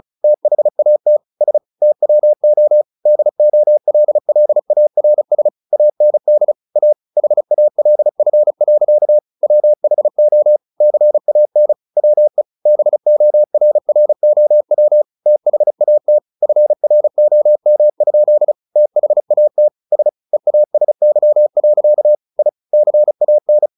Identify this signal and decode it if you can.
Morse code, heard as a keyed tone, about 34 wpm, 600 Hz. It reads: THAT S TWO DOLLARS AND A HALF. WHO CAN WE BORROW THAT FROM? THAT S EASY. I CAN
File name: the old man and the sea / the old man and the sea 1 35wpm